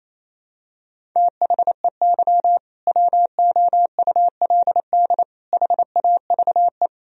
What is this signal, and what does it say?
Morse code, heard as a keyed tone, about 28 wpm, 700 Hz. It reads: THEY WOULD HAVE
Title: Sentences from Top 1000 Words - Review Part 1 - 28wpm